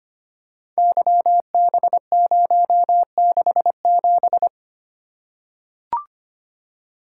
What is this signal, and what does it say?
Morse code, heard as a keyed tone, about 25 wpm, 700 Hz. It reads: YB067
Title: Code Groups - 5 Letter:Numbers - 25wpm